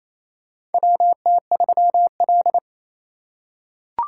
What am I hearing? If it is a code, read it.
Morse code, heard as a keyed tone, about 28 wpm, 700 Hz. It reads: WT3L
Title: USA Call Signs - 6x Character Spacing - 28wpm